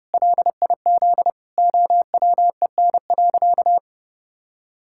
Morse code, heard as a keyed tone, about 30 wpm, 700 Hz. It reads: LIZ OWEN.
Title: Pangram Challenge v2 - 30wpm